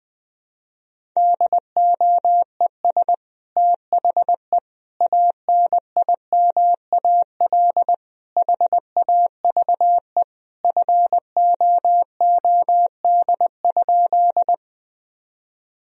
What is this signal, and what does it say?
Morse code, heard as a keyed tone, about 20 wpm, 700 Hz. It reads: DOES THE ANIMAL HAVE FOOD?
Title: Sentences from Top 1000 Words - Review Part 1 - 20wpm